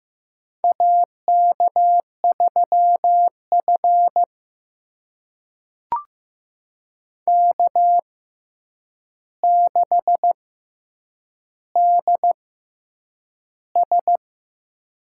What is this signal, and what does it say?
Morse code, heard as a keyed tone, about 15 wpm, spaced as for 8 wpm, 700 Hz. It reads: AK3F K6DS
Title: USA Call Signs - 4 Characters - 6x Character Spacing - 15wpm